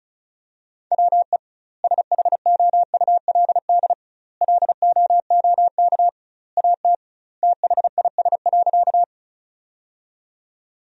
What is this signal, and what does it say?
Morse code, heard as a keyed tone, about 36 wpm, 700 Hz. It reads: WE SHOULD LOOK AT THIS.
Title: 2x Word-Spacing - Sentences from Top 300 Words - 35wpm